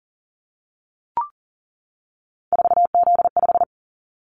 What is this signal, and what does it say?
Morse code, heard as a keyed tone, about 40 wpm, 700 Hz. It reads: 4Z5
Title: Callsign Prefixes for Top 50 DXCC - 40wpm